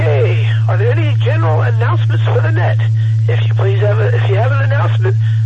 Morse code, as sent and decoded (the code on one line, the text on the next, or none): none